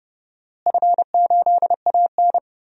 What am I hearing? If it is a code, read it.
Morse code, heard as a keyed tone, about 30 wpm, 700 Hz. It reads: F8AN
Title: DXCC-100 Call Signs - 4 Characters - Repeated 2x - 30wpm